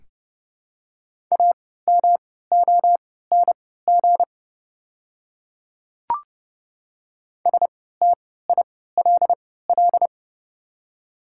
Morse code, heard as a keyed tone, about 30 wpm, spaced as for 17 wpm, 690 Hz. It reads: AMONG STILL
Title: 3x Character Spacing - Top 500 Words - 30wpm